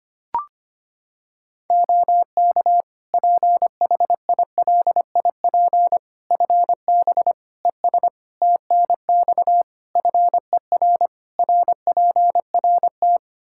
Morse code, heard as a keyed tone, about 25 wpm, 700 Hz. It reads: OK PHILIP FB ES TNX FER RPRT
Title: K1USN SST QSO Practice 25wpm